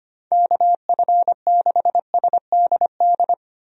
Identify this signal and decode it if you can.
Morse code, heard as a keyed tone, about 25 wpm, 700 Hz. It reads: KF6SDD
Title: USA Call Signs - 6 Characters - 25wpm